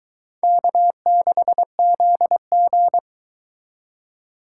Morse code, heard as a keyed tone, about 24 wpm, 700 Hz. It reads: K6ZG